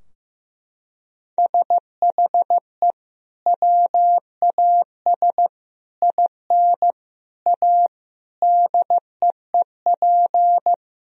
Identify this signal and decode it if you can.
Morse code, heard as a keyed tone, about 15 wpm, 700 Hz. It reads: SHE WAS IN A DEEP
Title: Sentences from Top 1000 Words - Review Part 3 - 15wpm